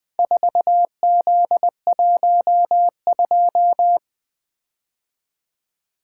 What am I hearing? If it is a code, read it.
Morse code, heard as a keyed tone, about 21 wpm, 700 Hz. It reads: 4Z12
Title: Four Letter-Number Combinations - 20wpm